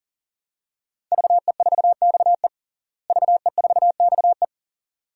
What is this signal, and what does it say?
Morse code, heard as a keyed tone, about 40 wpm, 700 Hz. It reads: VE4XE VE4XE
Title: DXCC-100 Call Signs - 5 Characters - Repeated 2x - 40wpm